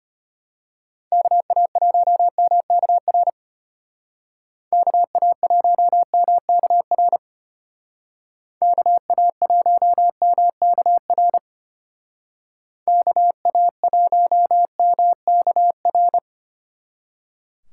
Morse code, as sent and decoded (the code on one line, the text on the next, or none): -.- .- .---- -- -.- .-. / -.- .- .---- -- -.- .-. / -.- .- .---- -- -.- .-. / -.- .- .---- -- -.- .-.
KA1MKR KA1MKR KA1MKR KA1MKR